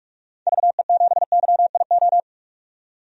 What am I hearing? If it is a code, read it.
Morse code, heard as a keyed tone, about 44 wpm, 690 Hz. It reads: VE7YIO
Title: DXCC-100 Call Signs - 6 Characters - 45wpm